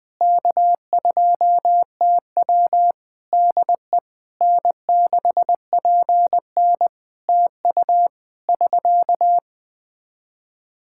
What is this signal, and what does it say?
Morse code, heard as a keyed tone, about 20 wpm, 700 Hz. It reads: K2TW DE N6PN TU <SK>
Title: Practice QSO - Exchange 4 - Ending - 20wpm